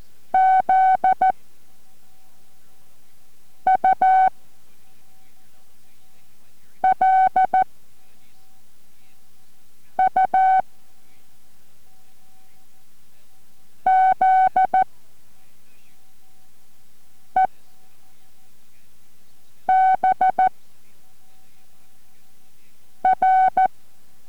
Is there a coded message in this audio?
Yes: Morse code from 0.3 to 23.7 s, heard as a keyed tone, about 14 wpm, 750 Hz. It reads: Z U L U Z E B R